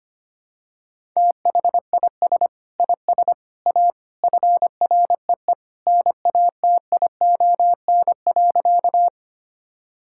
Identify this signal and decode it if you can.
Morse code, heard as a keyed tone, about 25 wpm, 700 Hz. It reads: THIS IS A FREE NATION.
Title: Sentences from Top 1000 Words - Review Part 6 - 25wpm